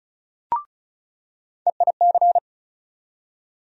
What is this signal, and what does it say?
Morse code, heard as a keyed tone, about 34 wpm, 700 Hz. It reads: EIC
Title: USA Call Sign Postfixes - 3 Letters 35wpm